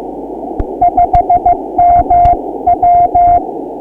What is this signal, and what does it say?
Morse code, heard as a keyed tone, about 15 wpm, 700 Hz, with noise. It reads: 5MW
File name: ARCHIV / Beacon 1999.65 / 2015 03 13small